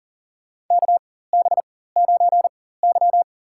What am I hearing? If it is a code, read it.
Morse code, heard as a keyed tone, about 40 wpm, spaced as for 19 wpm, 700 Hz. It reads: KB9Y
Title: USA Call Signs - 4x Character Spacing - 40wpm